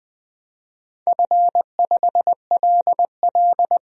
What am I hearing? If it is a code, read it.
Morse code, heard as a keyed tone, about 20 wpm, 700 Hz. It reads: F5LL
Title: DXCC-100 Call Signs - 4 Characters - 20wpm